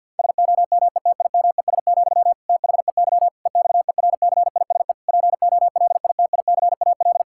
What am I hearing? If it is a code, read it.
Morse code, heard as a keyed tone, about 50 wpm, 700 Hz. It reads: SOMETIMES, THEY EXERCISE POLITICAL
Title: Sentences from Top 1000 Words - Review Part 6 - 50wpm